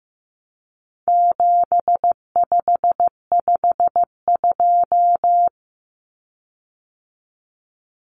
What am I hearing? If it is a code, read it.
Morse code, heard as a keyed tone, about 15 wpm, 700 Hz. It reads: 7552
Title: Common Cut Numbers - 4-Digit 15wpm